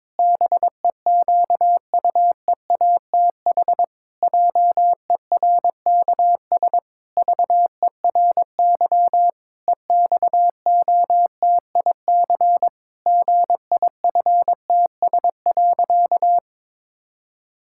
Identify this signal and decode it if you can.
Morse code, heard as a keyed tone, about 22 wpm, 700 Hz. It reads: BEQUEATH JERKS VERY EXOTIC GIFTS.
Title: Pangram Challenge v2 - 22wpm